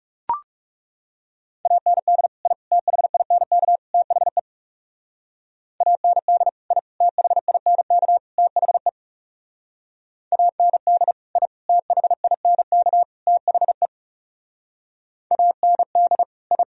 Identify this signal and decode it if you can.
Morse code, heard as a keyed tone, about 36 wpm, 700 Hz. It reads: AND I THINK THE AND I THINK THE AND I THINK THE AND I
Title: Speed-Racing - Sets of 4 Words 30wpm